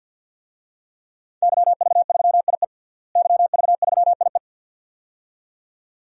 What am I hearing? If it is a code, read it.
Morse code, heard as a keyed tone, about 47 wpm, 700 Hz. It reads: YV3IE YV3IE
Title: DXCC-100 Call Signs - 5 Characters - Repeated 2x - 50wpm